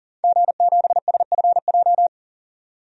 Morse code, heard as a keyed tone, about 40 wpm, 700 Hz. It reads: G7SFJ
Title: DXCC-100 Call Signs - 5 Characters - 40wpm